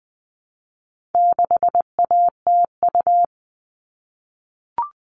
Morse code, heard as a keyed tone, about 20 wpm, 700 Hz. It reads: 6ATU
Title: All Cut Numbers - 4-Digit 20wpm